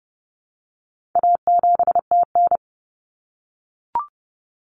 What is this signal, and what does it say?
Morse code, heard as a keyed tone, about 30 wpm, 700 Hz. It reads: A7TN
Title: Common Cut Numbers - 4-Digit 30wpm